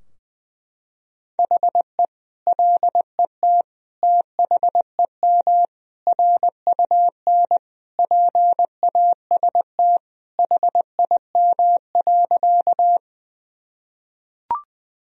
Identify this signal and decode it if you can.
Morse code, heard as a keyed tone, about 21 wpm, 700 Hz. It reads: HE LET THEM RUN PAST HIM.
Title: Sentences from Top 1000 Words - Review Part 3 - 20wpm